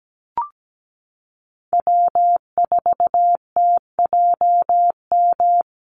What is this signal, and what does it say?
Morse code, heard as a keyed tone, about 17 wpm, 700 Hz. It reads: W4TJM
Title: USA Call Signs Repeated 2x 17wpm